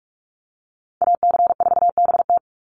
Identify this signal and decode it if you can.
Morse code, heard as a keyed tone, about 44 wpm, 700 Hz. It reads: AC4BT
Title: Call Signs Repeated 2x - 45wpm